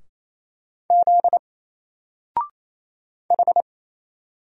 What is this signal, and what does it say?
Morse code, heard as a keyed tone, about 28 wpm, 700 Hz. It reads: Z H
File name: Single-Letter-Number-28wpm